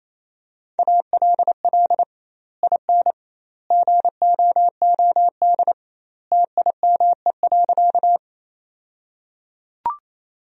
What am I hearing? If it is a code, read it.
Morse code, heard as a keyed tone, about 28 wpm, 700 Hz. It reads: ALL IN GOOD TIME.
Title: 2x Word-Spacing - Sentences from Top 200 Words - 28wpm